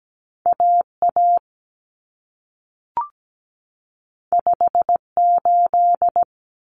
Morse code, heard as a keyed tone, about 17 wpm, 700 Hz. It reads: AA 58
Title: Common Cut Numbers - 2-Digit 17wpm